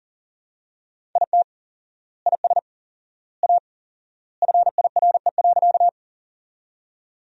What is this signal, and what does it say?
Morse code, heard as a keyed tone, about 40 wpm, 700 Hz. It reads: IT IS A FIRE.
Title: 4x Word-Spacing - Sentences from Top 400 Words - 40wpm